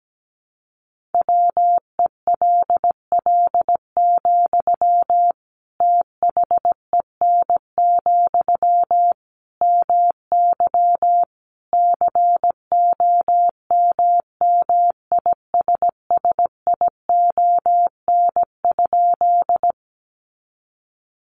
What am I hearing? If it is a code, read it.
Morse code, heard as a keyed tone, about 17 wpm, 700 Hz. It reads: WELL, THEN, MY COMMISSION?
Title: Sentences from Top 1400 Words 17wpm